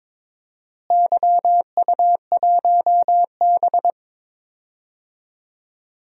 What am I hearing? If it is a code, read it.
Morse code, heard as a keyed tone, about 22 wpm, 700 Hz. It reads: YU1B
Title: DXCC-100 Call Signs - 4 Characters - Repeated 3x - 22wpm